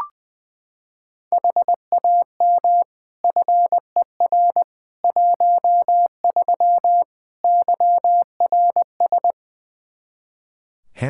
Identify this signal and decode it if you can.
Morse code, heard as a keyed tone, about 21 wpm, 700 Hz. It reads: HAM FER 13 YRS
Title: K1USN SST QSO Practice 20wpm